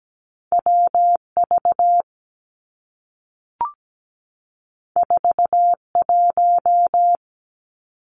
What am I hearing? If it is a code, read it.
Morse code, heard as a keyed tone, about 17 wpm, 700 Hz. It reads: W V 4 1
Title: Two Challenging Letter-Number Combinations 17wpm